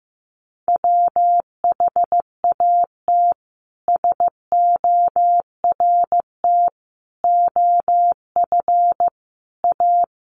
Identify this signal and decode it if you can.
Morse code, heard as a keyed tone, about 15 wpm, 700 Hz. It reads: WHAT SORT OF A
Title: Sentences from Top 1300 Words 15wpm